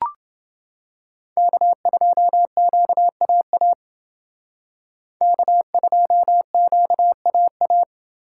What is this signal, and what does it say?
Morse code, heard as a keyed tone, about 28 wpm, 700 Hz. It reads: K2QAA K2QAA